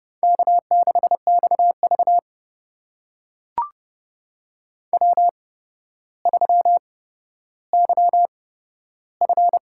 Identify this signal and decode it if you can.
Morse code, heard as a keyed tone, about 32 wpm, spaced as for 13 wpm, 700 Hz. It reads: K6XV W3YF
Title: USA Call Signs - 4 Characters - 8x Character Spacing - 30wpm